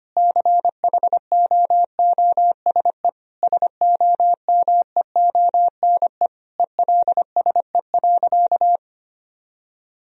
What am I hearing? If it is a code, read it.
Morse code, heard as a keyed tone, about 25 wpm, 700 Hz. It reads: CHOOSE SOMEONE ELSE.
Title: Sentences from Top 1000 Words - Review Part 4 - 25wpm